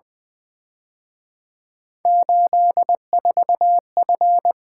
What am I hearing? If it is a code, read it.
Morse code, heard as a keyed tone, about 23 wpm, 700 Hz. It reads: 84F
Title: Speed-Racing - Three Letter-Number Combinations - 20wpm